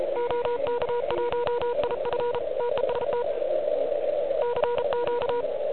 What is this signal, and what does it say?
Morse code, heard as a keyed tone, about 34 wpm, 960 Hz, with noise. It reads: OK1IF TEST CQ
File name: ARCHIV / QCX / QCX Pileup